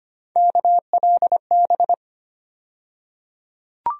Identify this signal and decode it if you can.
Morse code, heard as a keyed tone, about 25 wpm, 700 Hz. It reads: KLB
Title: USA Call Sign Postfixes - 3 Letters 25wpm